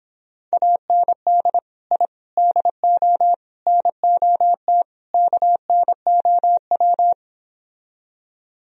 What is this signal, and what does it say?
Morse code, heard as a keyed tone, about 29 wpm, 700 Hz. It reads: AND I DO NOT KNOW